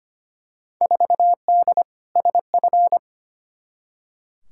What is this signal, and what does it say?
Morse code, heard as a keyed tone, about 25 wpm, 700 Hz. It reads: 4D SF
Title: Field Day Exchanges Encore - 25wpm